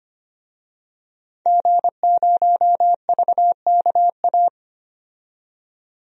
Morse code, heard as a keyed tone, about 25 wpm, 700 Hz. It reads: G0VKA
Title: DXCC-100 Call Signs - 5 Characters - 25wpm